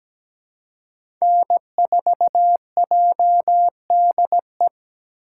Morse code, heard as a keyed tone, about 17 wpm, 700 Hz. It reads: N4JDE
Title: USA Call Signs - 5 Characters - 4x Character Spacing - 17wpm